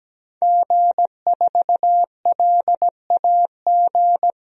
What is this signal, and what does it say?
Morse code, heard as a keyed tone, about 17 wpm, 700 Hz. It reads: G4LAG
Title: DXCC-100 Call Signs - 5 Characters - 17wpm